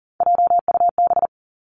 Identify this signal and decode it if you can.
Morse code, heard as a keyed tone, about 38 wpm, 700 Hz. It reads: JUB
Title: Three Challenging Letter-Number Combinations 40wpm